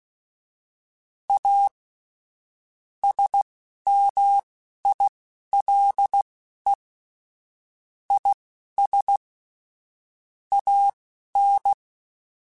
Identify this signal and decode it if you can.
Morse code, heard as a keyed tone, about 16 wpm, spaced as for 11 wpm, 800 Hz. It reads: A SMILE IS AN
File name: Quotes / Quotes-mp3s / A Smile @10